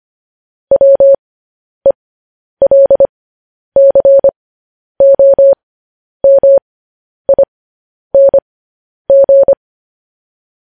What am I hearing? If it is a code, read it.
Morse code, heard as a keyed tone, about 25 wpm, spaced as for 10 wpm, 550 Hz. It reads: WELCOMING
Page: news-headlines-in-morse-code-at-10-wpm